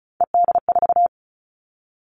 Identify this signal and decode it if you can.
Morse code, heard as a keyed tone, about 34 wpm, 700 Hz. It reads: ED4
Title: All Cut Numbers - 3-Digit 35wpm